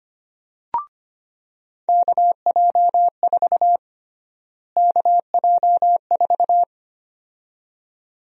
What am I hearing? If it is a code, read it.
Morse code, heard as a keyed tone, about 25 wpm, 700 Hz. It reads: KJ4 KJ4
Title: USA Call Sign Prefixes - Repeated 2x - 25wpm